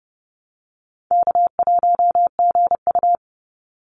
Morse code, heard as a keyed tone, about 30 wpm, 700 Hz. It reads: K1GU
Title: Call Signs - 30wpm